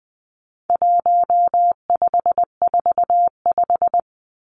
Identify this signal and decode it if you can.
Morse code, heard as a keyed tone, about 20 wpm, 700 Hz. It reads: 1545
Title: Numbers - 4-Digit 20wpm